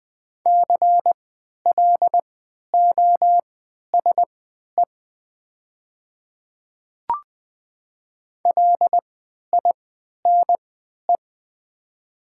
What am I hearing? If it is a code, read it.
Morse code, heard as a keyed tone, about 20 wpm, spaced as for 11 wpm, 700 Hz. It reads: CLOSE LINE
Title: 3x Character Spacing - Top 300 Words - 20wpm